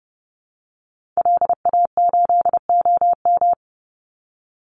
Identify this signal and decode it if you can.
Morse code, heard as a keyed tone, about 30 wpm, 700 Hz. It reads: LA8OM
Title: Call Signs - 30wpm